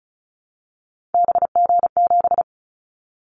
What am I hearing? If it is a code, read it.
Morse code, heard as a keyed tone, about 36 wpm, 700 Hz. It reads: BG7
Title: Callsign Prefixes for Top 100 DXCC - 35wpm